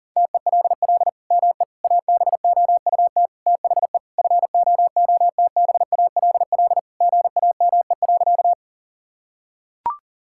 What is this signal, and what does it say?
Morse code, heard as a keyed tone, about 40 wpm, 700 Hz. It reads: TELL ME ABOUT THE FOOTBALL GAME.
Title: Sentences from Top 1600 Words 40wpm